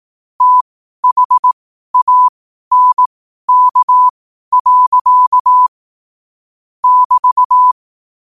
Morse code, heard as a keyed tone, about 18 wpm, spaced as for 12 wpm, 1000 Hz. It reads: THANK. <BT>
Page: Quotes for Fri, 15 Aug 2025 in Morse Code at 12 words per minute.